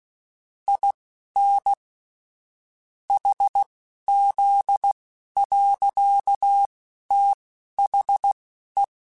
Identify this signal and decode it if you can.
Morse code, heard as a keyed tone, about 16 wpm, spaced as for 11 wpm, 800 Hz. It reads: IN HZ. THE
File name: Technical Subjects / Technical Subjects-mp3s / XC @10